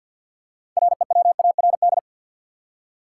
Morse code, heard as a keyed tone, about 48 wpm, 700 Hz. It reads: REWARD
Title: Six Letter Words - 50wpm